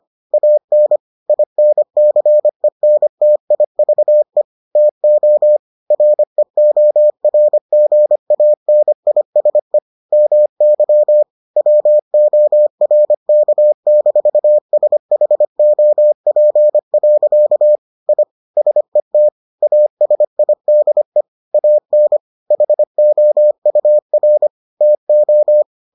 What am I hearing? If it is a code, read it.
Morse code, heard as a keyed tone, about 25 wpm, 600 Hz. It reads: AN INCENTIVE TO REORGANISE MY WORK-SHOP. I SET ASIDE AN HOUR TO